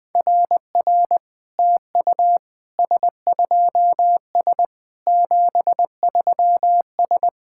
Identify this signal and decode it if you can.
Morse code, heard as a keyed tone, about 21 wpm, 700 Hz. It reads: RR TU S2S 73S
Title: SOTA QSO Practice 20wpm